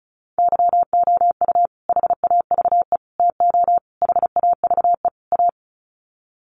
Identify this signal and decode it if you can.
Morse code, heard as a keyed tone, about 34 wpm, 700 Hz. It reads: YOU HAVE TO HAVE A
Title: Sets of 5 Words Encore 35wpm